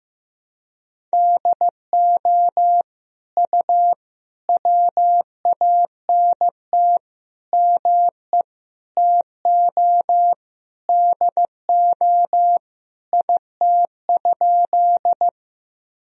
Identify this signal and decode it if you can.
Morse code, heard as a keyed tone, about 15 wpm, 700 Hz. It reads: DO U WANT ME TO DO IT?
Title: Sentences from Top 200 Words and Abbreviations 15wpm